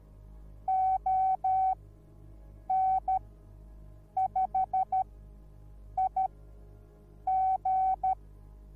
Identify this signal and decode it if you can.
Morse code, heard as a keyed tone, about 13 wpm, spaced as for 7 wpm, 760 Hz. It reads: ON5IG